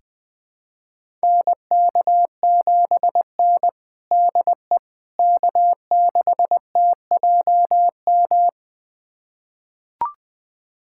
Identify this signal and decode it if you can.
Morse code, heard as a keyed tone, about 21 wpm, 700 Hz. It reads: NK7N DE K6TJM
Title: USA Callsign DE Callsign 20wpm